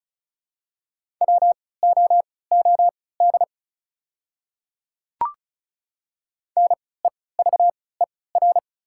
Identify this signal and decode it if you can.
Morse code, heard as a keyed tone, about 36 wpm, spaced as for 20 wpm, 690 Hz. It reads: WOOD NEVER
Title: 3x Character Spacing - Top 500 Words - 35wpm